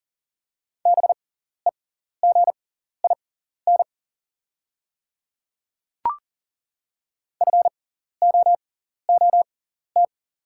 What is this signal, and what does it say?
Morse code, heard as a keyed tone, about 40 wpm, spaced as for 14 wpm, 700 Hz. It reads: BEGIN FOOT
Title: 6x Character Spacing - Top 500 Words - 40wpm